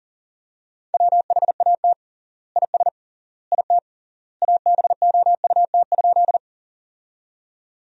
Morse code, heard as a keyed tone, about 40 wpm, 700 Hz. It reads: WHAT IS IT ABOUT?
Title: 3x Word-Spacing - Sentences from Top 100 Words - 40wpm